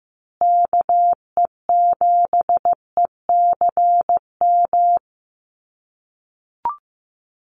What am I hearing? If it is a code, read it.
Morse code, heard as a keyed tone, about 15 wpm, 700 Hz. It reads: KE7ECM
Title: USA Call Signs Repeated 2x 15wpm